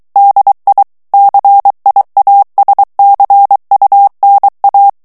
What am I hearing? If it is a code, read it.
Morse code, heard as a keyed tone, about 23 wpm, 800 Hz. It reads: DI CIASCUNA
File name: ITALIEN texte clair 2 vitesse 22 mots minute